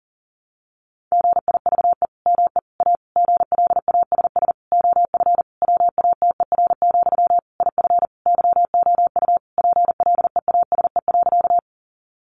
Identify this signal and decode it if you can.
Morse code, heard as a keyed tone, about 40 wpm, 700 Hz. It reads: GIVE ME A GLASS OF WATER, IF YOU PLEASE.
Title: Sentences from Top 1500 Words 40wpm